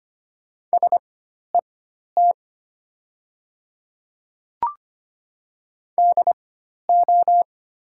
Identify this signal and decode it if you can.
Morse code, heard as a keyed tone, about 25 wpm, spaced as for 12 wpm, 690 Hz. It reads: SET DO
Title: 4x Character Spacing - Top 200 Words - 25wpm